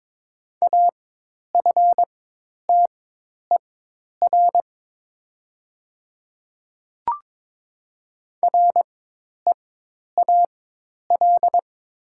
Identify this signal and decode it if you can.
Morse code, heard as a keyed tone, about 22 wpm, spaced as for 10 wpm, 700 Hz. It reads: AFTER REAL
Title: 4x Character Spacing - Top 300 Words - 22wpm